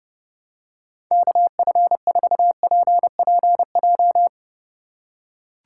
Morse code, heard as a keyed tone, about 30 wpm, 700 Hz. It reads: KF4PPJ
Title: USA Call Signs - 4x Character Spacing - 30wpm